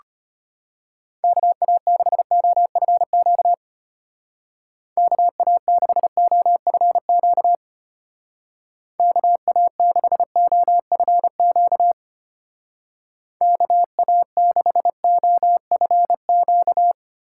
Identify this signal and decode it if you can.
Morse code, heard as a keyed tone, about 31 wpm, 700 Hz. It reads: KA6OFQ KA6OFQ KA6OFQ KA6OFQ
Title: Speed Racing - USA 2x3 Call Signs 25wpm